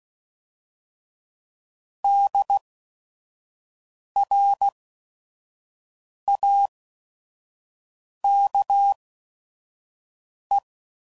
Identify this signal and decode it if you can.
Morse code, heard as a keyed tone, about 16 wpm, spaced as for 5 wpm, 800 Hz. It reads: DRAKE
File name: Prefixes / By First Letter / Ds